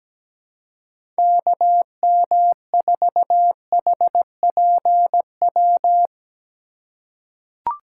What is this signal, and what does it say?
Morse code, heard as a keyed tone, about 17 wpm, 700 Hz. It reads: KM4HPW
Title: USA Call Signs - 6 Characters - 2x Character Spacing - 17wpm